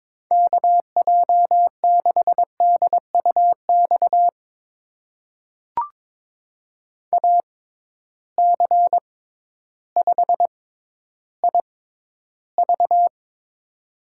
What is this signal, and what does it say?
Morse code, heard as a keyed tone, about 22 wpm, spaced as for 12 wpm, 700 Hz. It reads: KJ6DUX AC5IV
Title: USA Call Signs - 6x Character Spacing - 22wpm